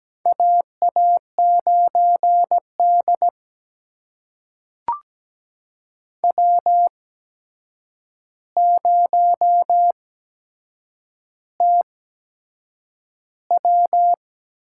Morse code, heard as a keyed tone, about 17 wpm, spaced as for 7 wpm, 700 Hz. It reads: AA9D W0TW
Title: USA Call Signs - 4 Characters - 8x Character Spacing - 17wpm